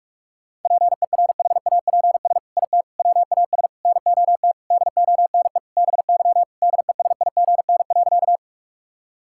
Morse code, heard as a keyed tone, about 41 wpm, 700 Hz. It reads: PERHAPS IT WAS NOT DONE BY DESIGN.
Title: Sentences from Top 1000 Words - Review Part 5 - 45wpm